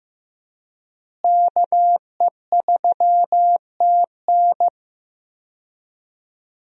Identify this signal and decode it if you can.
Morse code, heard as a keyed tone, about 16 wpm, 700 Hz. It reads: KE3TN
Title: USA Call Signs - 5 Characters - Repeated 3x - 15wpm